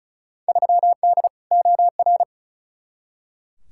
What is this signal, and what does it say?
Morse code, heard as a keyed tone, about 36 wpm, 700 Hz. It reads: 3D OR
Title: Field Day Exchanges - 35wpm